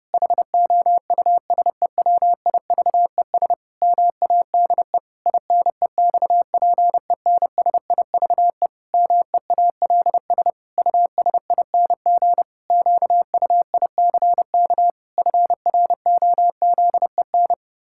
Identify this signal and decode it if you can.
Morse code, heard as a keyed tone, about 30 wpm, 700 Hz. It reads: HOUSEWIVES MADE INEXPENSIVE MEALS USING QUICK FROZEN
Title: Pangram Challenge v2 - 30wpm